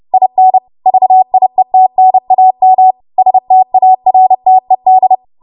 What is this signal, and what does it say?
Morse code, heard as a keyed tone, about 30 wpm, 750 Hz. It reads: IN VIETNAM STARTED
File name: Archive / 30 WPM / 30WPM